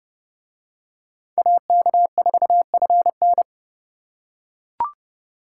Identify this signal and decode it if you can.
Morse code, heard as a keyed tone, about 27 wpm, 700 Hz. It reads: AK4FN